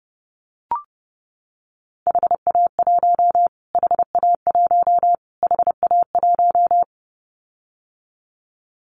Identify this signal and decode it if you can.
Morse code, heard as a keyed tone, about 30 wpm, 700 Hz. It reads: HA1 HA1 HA1
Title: Callsign Prefixes for Top 25 DXCC - Repeated 3x - 30wpm